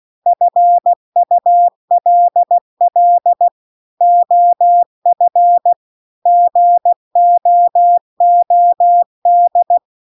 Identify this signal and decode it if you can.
Morse code, heard as a keyed tone, about 16 wpm, 700 Hz. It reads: FULL OF GOOD